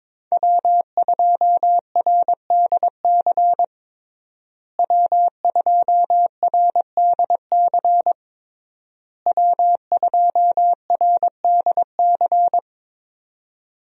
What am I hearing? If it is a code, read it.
Morse code, heard as a keyed tone, about 22 wpm, 700 Hz. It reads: W2RDC W2RDC W2RDC
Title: USA Call Signs - 5 Characters - Repeated 3x - 22wpm